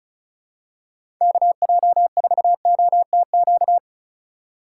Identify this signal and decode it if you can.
Morse code, heard as a keyed tone, about 34 wpm, 700 Hz. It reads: KJ4OTQ
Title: USA Call Signs - 6 Characters - 4x Character Spacing - 35wpm